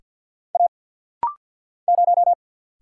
Morse code, heard as a keyed tone, about 50 wpm, 700 Hz. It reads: A 0
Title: Single Letter-Number - Rapid-Fire 50wpm